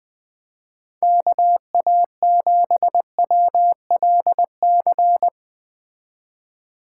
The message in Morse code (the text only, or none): KA7WLC